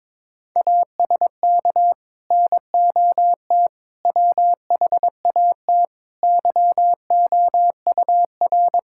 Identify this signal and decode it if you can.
Morse code, heard as a keyed tone, about 22 wpm, 700 Hz. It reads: ASK NOT WHAT YOUR
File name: Sets of 9 Words - 22wpm